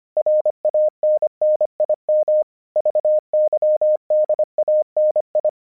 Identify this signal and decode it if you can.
Morse code, heard as a keyed tone, about 25 wpm, 610 Hz. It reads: RANNIM VYDANI
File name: vyuka cw 1 / 25wpm / hasek 25wpm uvod